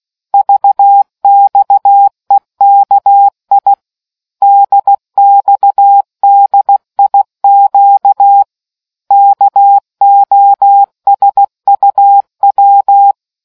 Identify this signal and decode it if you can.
Morse code, heard as a keyed tone, about 16 wpm, 800 Hz. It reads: VXEKI DXDIQ KOSUW